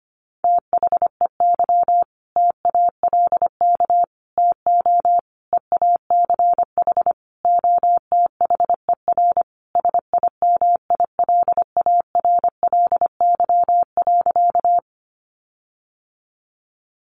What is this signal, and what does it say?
Morse code, heard as a keyed tone, about 25 wpm, 700 Hz. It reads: THEY TALK TO EACH OTHER SIMILARLY.
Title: Sentences from Top 1400 Words 25wpm